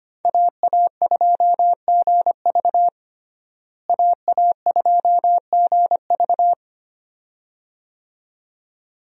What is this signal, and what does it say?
Morse code, heard as a keyed tone, about 25 wpm, 700 Hz. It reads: AA2GV AA2GV
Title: USA Call Signs - 5 Characters - Repeated 2x - 25wpm